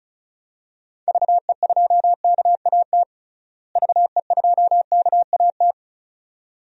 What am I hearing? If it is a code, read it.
Morse code, heard as a keyed tone, about 36 wpm, 700 Hz. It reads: VE2KAT VE2KAT
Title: DXCC-100 Call Signs - 6 Characters - Repeated 2x - 35wpm